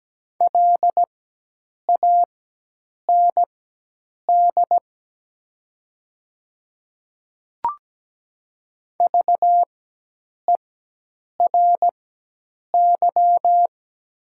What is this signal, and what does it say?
Morse code, heard as a keyed tone, about 17 wpm, spaced as for 8 wpm, 700 Hz. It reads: LAND VERY
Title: 4x Character Spacing - Top 200 Words - 17wpm